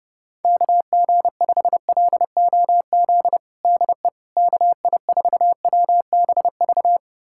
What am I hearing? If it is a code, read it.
Morse code, heard as a keyed tone, about 29 wpm, 700 Hz. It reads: KG5LOZ DE KI4WBV
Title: USA Callsign DE Callsign - 6 Characters - 30wpm